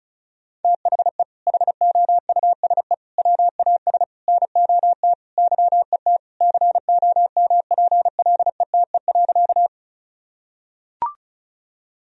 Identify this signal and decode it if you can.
Morse code, heard as a keyed tone, about 36 wpm, 700 Hz. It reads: THE HOUSE WAS NOT YET COMPLETE.
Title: Sentences from Top 1000 Words - Review Part 2 - 35wpm